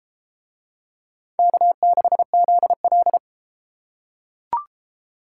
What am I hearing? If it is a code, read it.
Morse code, heard as a keyed tone, about 32 wpm, 700 Hz. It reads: K6ZL